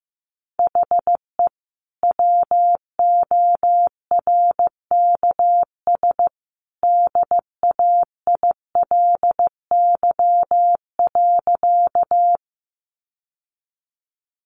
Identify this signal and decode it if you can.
Morse code, heard as a keyed tone, about 15 wpm, 700 Hz. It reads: HE WORKS DAILY.
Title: Sentences from Top 1100 Words 15wpm